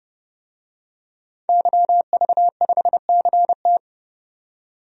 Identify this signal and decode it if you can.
Morse code, heard as a keyed tone, about 31 wpm, 700 Hz. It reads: YV5CT
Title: DXCC-100 Call Signs - 5 Characters - 30wpm